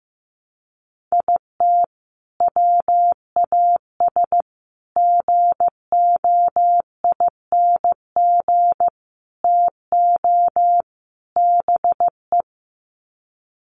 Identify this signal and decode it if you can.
Morse code, heard as a keyed tone, about 15 wpm, 700 Hz. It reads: IT WAS GOING TO BE
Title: Sets of 5 Words 15wpm